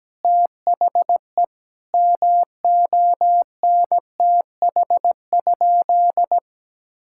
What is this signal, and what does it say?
Morse code, heard as a keyed tone, about 17 wpm, 700 Hz. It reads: THE MONTH?
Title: Sentences from Top 1000 Words - Review Part 6 - 17wpm